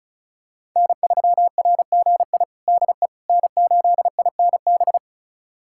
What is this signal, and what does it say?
Morse code, heard as a keyed tone, about 34 wpm, 700 Hz. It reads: N3RGI DE N8INB
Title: USA Callsign DE Callsign - 5 Characters - 35wpm